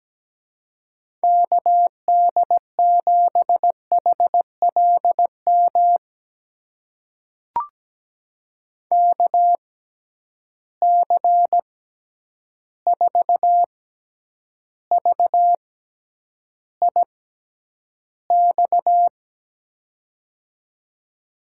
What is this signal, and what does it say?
Morse code, heard as a keyed tone, about 17 wpm, spaced as for 9 wpm, 700 Hz. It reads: KD7HLM KC4VIX
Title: USA Call Signs - 6 Characters - 6x Character Spacing - 17wpm